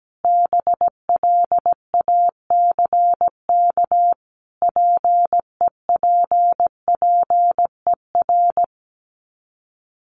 Morse code, heard as a keyed tone, about 17 wpm, 690 Hz. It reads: BLACK PEPPER
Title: Sets of 5 Words Encore 17wpm